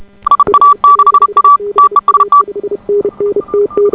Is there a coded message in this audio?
Yes: Morse code from 0.3 to 2.4 s, heard as a keyed tone, about 33 wpm, 1150 Hz. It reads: IA6A TEST